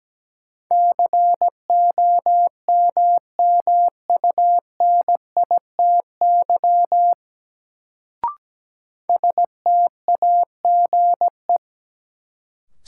Morse code, heard as a keyed tone, about 17 wpm, 700 Hz. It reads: COMMUNITY STAGE
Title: Sentences from Top 1200 Words - New Word Drill Alt - 17wpm